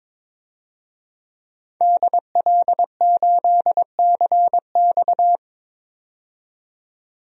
Morse code, heard as a keyed tone, about 22 wpm, 700 Hz. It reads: DL8CX
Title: DXCC-100 Call Signs - 5 Characters - 22wpm